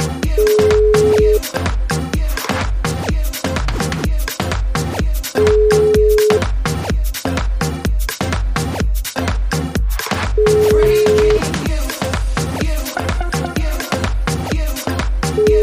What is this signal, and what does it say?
Ringback tone (Europe, 425 Hz), with noise.